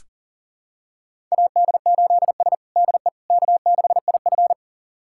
Morse code, heard as a keyed tone, about 37 wpm, 700 Hz. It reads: AD8S DE K6IF
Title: USA Callsign DE Callsign - 4 Characters - 40wpm